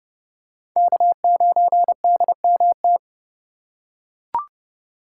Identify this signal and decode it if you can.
Morse code, heard as a keyed tone, about 31 wpm, 700 Hz. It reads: K9DMT
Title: USA Call Signs - 5 Characters - 4x Character Spacing - 30wpm